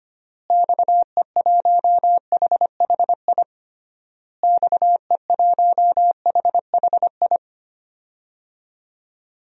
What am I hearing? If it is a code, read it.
Morse code, heard as a keyed tone, about 25 wpm, 700 Hz. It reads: XE1HHI XE1HHI
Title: DXCC-100 Call Signs - 6 Characters - Repeated 3x - 25wpm